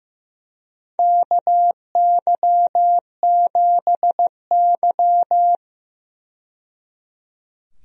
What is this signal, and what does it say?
Morse code, heard as a keyed tone, about 15 wpm, 700 Hz. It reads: KY7Y
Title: USA Call Signs - 4 Characters - 15wpm